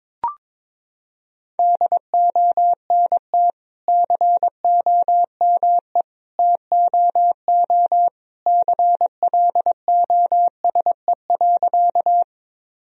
Morse code, heard as a keyed tone, about 21 wpm, 700 Hz. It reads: DONT COME TOO CLOSE.
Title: Sentences from Top 1000 Words - Review Part 1 - 22wpm